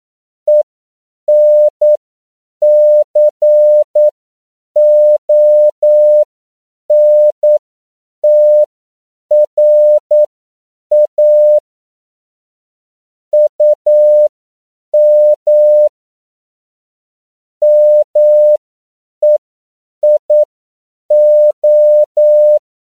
Morse code, heard as a keyed tone, about 9 wpm, spaced as for 7 wpm, 600 Hz. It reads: ENCONTRA UM MEIO
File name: CW / 10PPM